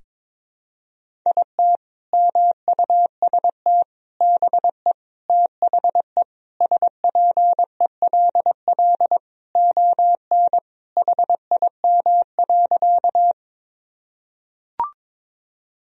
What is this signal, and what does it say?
Morse code, heard as a keyed tone, about 22 wpm, 700 Hz. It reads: IT MUST BE THE SPELL ON HIM.
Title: Sentences from Top 1000 Words - Review Part 1 - 22wpm